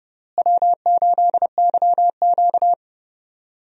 Morse code, heard as a keyed tone, about 32 wpm, 700 Hz. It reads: W8YQ
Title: USA Call Signs - 4 Characters - Repeated 3x - 30wpm